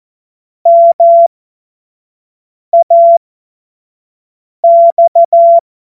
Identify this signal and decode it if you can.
Morse code, heard as a keyed tone, about 14 wpm, spaced as for 5 wpm, 680 Hz. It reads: MAX